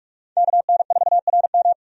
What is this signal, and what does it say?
Morse code, heard as a keyed tone, about 43 wpm, 700 Hz. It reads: KN4RM
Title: USA Call Signs - 5 Characters - 45wpm